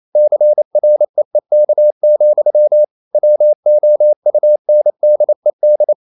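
Morse code, heard as a keyed tone, about 28 wpm, 600 Hz. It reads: CREEK, WOUNDED
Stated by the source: Morse code practice at 28 wpm, text of Ken Follett's novel Never in English, from CW med Gnister.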